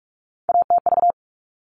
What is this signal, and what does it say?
Morse code, heard as a keyed tone, about 44 wpm, 700 Hz. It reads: ATV
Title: All Cut Numbers - 3-Digit 45wpm